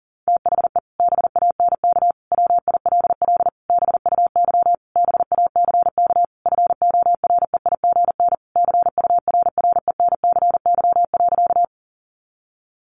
Morse code, heard as a keyed tone, about 40 wpm, 700 Hz. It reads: THE BANK WILL BUY BACK FOREIGN CURRENCY.
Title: Sentences from Top 1500 Words 40wpm